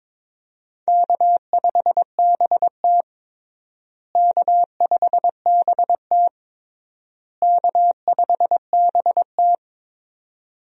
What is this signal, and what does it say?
Morse code, heard as a keyed tone, about 22 wpm, 700 Hz. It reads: K5BT K5BT K5BT
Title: USA Call Signs - 4 Characters - Repeated 3x - 22wpm